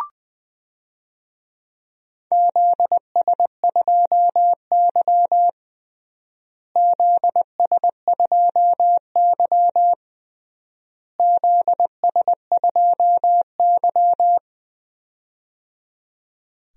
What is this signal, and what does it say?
Morse code, heard as a keyed tone, about 21 wpm, 690 Hz. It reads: ZS2Y ZS2Y ZS2Y
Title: DXCC-100 Call Signs - 4 Characters - Repeated 3x - 20wpm